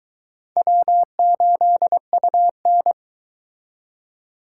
Morse code, heard as a keyed tone, about 23 wpm, 700 Hz. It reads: W8UN